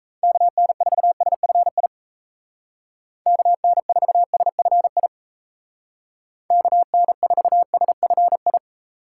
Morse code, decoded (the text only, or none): KN4SFI KN4SFI KN4SFI